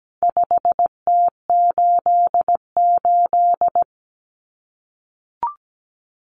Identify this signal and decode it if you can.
Morse code, heard as a keyed tone, about 17 wpm, 700 Hz. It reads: 5T88
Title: Common Cut Numbers - 4-Digit 17wpm